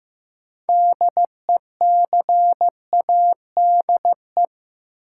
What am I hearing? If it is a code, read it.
Morse code, heard as a keyed tone, about 15 wpm, 700 Hz. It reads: DECADE
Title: Sets of 5 Words - New Word Drill - 15wpm